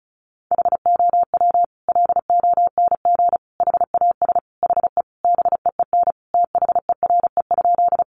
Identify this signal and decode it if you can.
Morse code, heard as a keyed tone, about 34 wpm, 700 Hz. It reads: HOW LONG HAS HE BEEN THERE?
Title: Sentences from Top 100 Words 35wpm